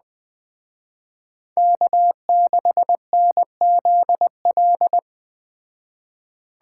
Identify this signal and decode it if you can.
Morse code, heard as a keyed tone, about 20 wpm, 700 Hz. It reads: K6NZL
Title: Speed Racing - USA 1x3 Call Signs 17wpm